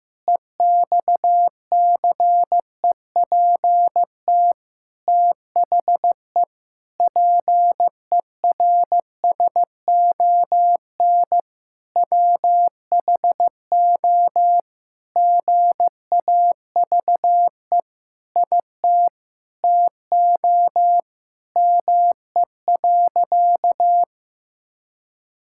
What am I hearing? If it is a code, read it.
Morse code, heard as a keyed tone, about 15 wpm, 700 Hz. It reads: EXCEPT THE PERSON WHO GAVE IT TO ME.
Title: Sentences from Top 1600 Words 15wpm